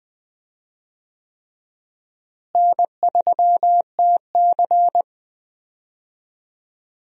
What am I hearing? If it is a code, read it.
Morse code, heard as a keyed tone, about 20 wpm, 700 Hz. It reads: N3TC
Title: USA Call Signs - 4 Characters - Repeated 3x - 20wpm